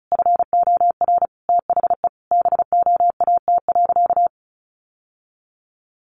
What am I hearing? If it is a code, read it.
Morse code, heard as a keyed tone, about 34 wpm, 700 Hz. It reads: FOR THE BOAT.
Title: Sentences from Top 1100 Words 35wpm